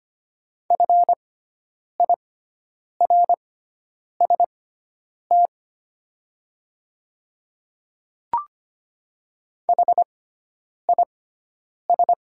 Morse code, heard as a keyed tone, about 25 wpm, spaced as for 9 wpm, 700 Hz. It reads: FIRST HIS
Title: 6x Character Spacing - Top 100 Words - 25wpm